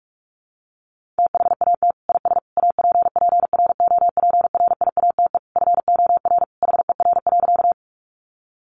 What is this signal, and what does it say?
Morse code, heard as a keyed tone, about 44 wpm, 700 Hz. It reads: THAT IS APPROPRIATE FOR HER.
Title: Sentences from Top 900 Words 45wpm